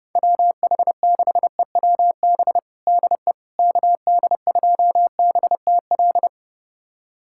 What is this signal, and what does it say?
Morse code, heard as a keyed tone, about 30 wpm, 690 Hz. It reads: WH6EWB DE KD2BTL
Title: USA Callsign DE Callsign - 6 Characters - 30wpm